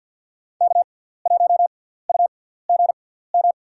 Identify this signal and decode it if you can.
Morse code, heard as a keyed tone, about 53 wpm, spaced as for 17 wpm, 700 Hz. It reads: K1UGM
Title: USA Call Signs - 6x Character Spacing - 50wpm